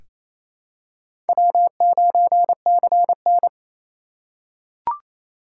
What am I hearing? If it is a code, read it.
Morse code, heard as a keyed tone, about 28 wpm, 700 Hz. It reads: W9CN
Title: USA Call Signs - 4 Characters - 3x Character Spacing - 28wpm